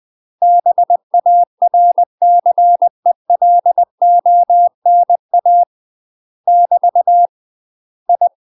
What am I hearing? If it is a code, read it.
Morse code, heard as a keyed tone, about 20 wpm, 700 Hz. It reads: BARCELONA <BT> I
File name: UK MASTER QSOs TRAINING / 20WPM / UK MASTER QSOs 0069